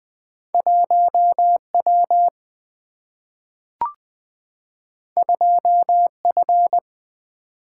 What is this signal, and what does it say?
Morse code, heard as a keyed tone, about 20 wpm, 700 Hz. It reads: 1W 2F
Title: Single-Letter Single-Number 20wpm